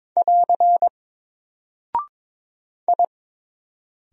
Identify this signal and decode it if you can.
Morse code, heard as a keyed tone, about 22 wpm, 700 Hz. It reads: <AR> I
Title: Single Character (Letter:Number:Punctuation:Prosign) 22wpm